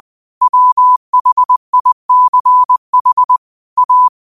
Morse code, heard as a keyed tone, about 20 wpm, 1000 Hz. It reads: WHICH A
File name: qotdfiles / 20wpm